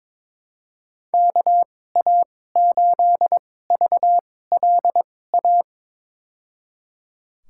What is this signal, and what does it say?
Morse code, heard as a keyed tone, about 22 wpm, spaced as for 16 wpm, 700 Hz. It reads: KA8VLA
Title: USA Call Signs - 2x Character Spacing - 22wpm